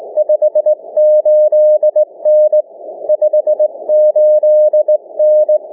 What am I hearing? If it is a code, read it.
Morse code, heard as a keyed tone, about 20 wpm, spaced as for 18 wpm, 610 Hz. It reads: 58N 58N
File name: cw / ssn 30M